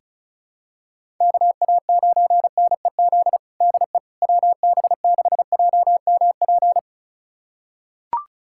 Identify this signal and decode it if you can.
Morse code, heard as a keyed tone, about 34 wpm, 700 Hz. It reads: KA9NEZ DE WB6JMP
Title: USA Callsign DE Callsign 35wpm